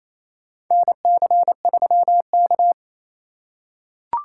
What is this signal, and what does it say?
Morse code, heard as a keyed tone, about 28 wpm, 700 Hz. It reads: NC3K
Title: USA Call Signs - 2x Character Spacing - 28wpm